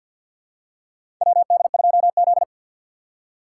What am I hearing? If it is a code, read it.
Morse code, heard as a keyed tone, about 53 wpm, 700 Hz. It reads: WD2Z
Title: USA Call Signs - 4 Characters - 6x Character Spacing - 50wpm